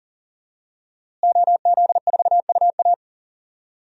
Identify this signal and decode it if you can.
Morse code, heard as a keyed tone, about 40 wpm, 700 Hz. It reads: OZ4UA
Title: DXCC-100 Call Signs - 5 Characters - Repeated 2x - 40wpm